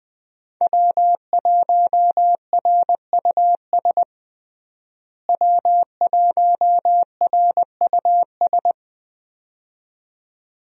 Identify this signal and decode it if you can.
Morse code, heard as a keyed tone, about 20 wpm, 700 Hz. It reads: W1RUS W1RUS
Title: USA Call Signs - 5 Characters - Repeated 2x - 20wpm